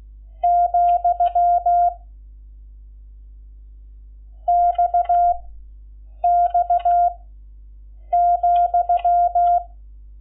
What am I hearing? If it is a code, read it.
Morse code, heard as a keyed tone, about 16 wpm, 700 Hz. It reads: , X X ,
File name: Lecciones / CLASE Nº11